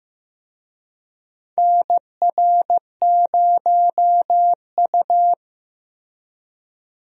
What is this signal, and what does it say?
Morse code, heard as a keyed tone, about 15 wpm, 700 Hz. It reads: NR0U
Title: USA Call Signs - 4 Characters - Repeated 3x - 15wpm